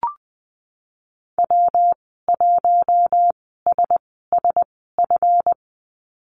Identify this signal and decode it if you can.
Morse code, heard as a keyed tone, about 19 wpm, spaced as for 14 wpm, 700 Hz. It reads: W1SSF
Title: 147 Lesson 24g - Letter y - Call Signs 20wpm